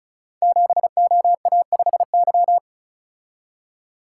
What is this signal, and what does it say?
Morse code, heard as a keyed tone, about 36 wpm, 700 Hz. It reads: 7OA5Y
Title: Code Groups - 5 Letter:Numbers - 35wpm